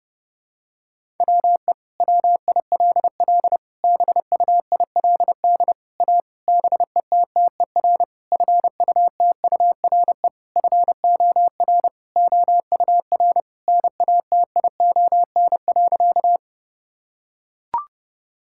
Morse code, heard as a keyed tone, about 32 wpm, 700 Hz. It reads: WE WILL BUILD A BETTER FUTURE FOR OUR NATION.
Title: Sentences from Top 1000 Words - Review Part 6 - 30wpm